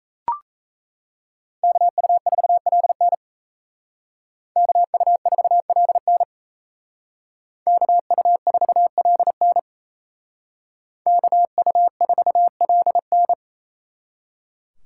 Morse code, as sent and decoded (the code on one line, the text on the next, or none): -.- ..- ....- .-.. -. / -.- ..- ....- .-.. -. / -.- ..- ....- .-.. -. / -.- ..- ....- .-.. -.
KU4LN KU4LN KU4LN KU4LN